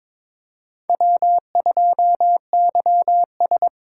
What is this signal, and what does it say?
Morse code, heard as a keyed tone, about 26 wpm, 700 Hz. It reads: W2YS
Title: Speed Racing - USA Call Signs 22wpm